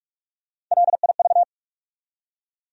Morse code, heard as a keyed tone, about 46 wpm, 700 Hz. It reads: LIV
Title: USA Common Names - Max 3 Letters - 45wpm